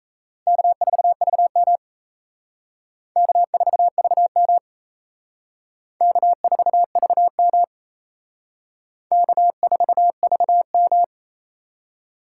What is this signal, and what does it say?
Morse code, heard as a keyed tone, about 36 wpm, 700 Hz. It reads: K4VM K4VM K4VM K4VM